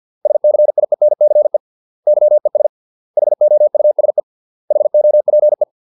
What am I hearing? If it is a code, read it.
Morse code, heard as a keyed tone, about 50 wpm, 600 Hz. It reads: SCIENCE YES HOUSE HOPE